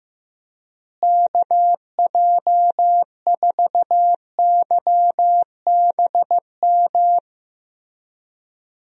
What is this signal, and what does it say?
Morse code, heard as a keyed tone, about 15 wpm, 700 Hz. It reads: KJ4YBM
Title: USA Call Signs - 6 Characters - Repeated 2x - 15wpm